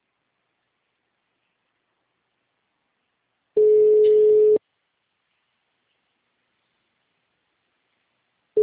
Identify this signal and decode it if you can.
Ringback tone (Europe, 425 Hz).
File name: out- -2010- - - .5990